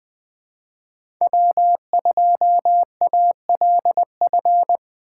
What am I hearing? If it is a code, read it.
Morse code, heard as a keyed tone, about 24 wpm, 700 Hz. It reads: W2ALF
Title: Speed Racing - USA 1x3 Call Signs 20wpm